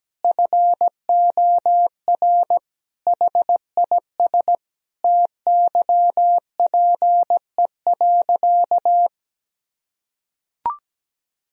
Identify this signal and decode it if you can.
Morse code, heard as a keyed tone, about 17 wpm, 700 Hz. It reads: FOR HIS TYPE.
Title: Sentences from Top 1000 Words - Review Part 3 - 17wpm